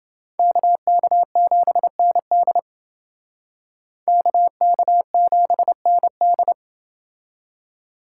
Morse code, heard as a keyed tone, about 28 wpm, 700 Hz. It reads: KK7ND KK7ND